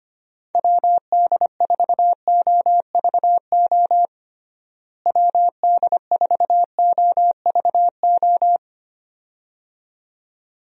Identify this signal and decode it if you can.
Morse code, heard as a keyed tone, about 25 wpm, 700 Hz. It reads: WD4OVO WD4OVO
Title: USA Call Signs - 6 Characters - Repeated 2x - 25wpm